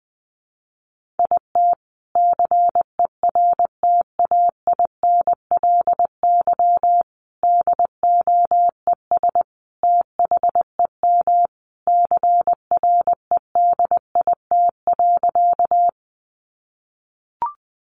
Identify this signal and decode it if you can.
Morse code, heard as a keyed tone, about 20 wpm, 700 Hz. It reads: IT CERTAINLY DOES THEM CREDIT.
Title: Sentences from Top 1000 Words 20wpm